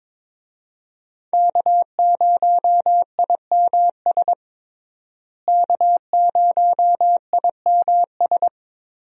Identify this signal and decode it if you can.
Morse code, heard as a keyed tone, about 22 wpm, 700 Hz. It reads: K0IMS K0IMS
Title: USA Call Signs - 5 Characters - Repeated 2x - 22wpm